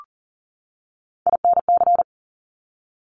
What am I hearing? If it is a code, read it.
Morse code, heard as a keyed tone, about 38 wpm, 700 Hz. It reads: INC
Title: Top Three Letter Combinations in Words 40wpm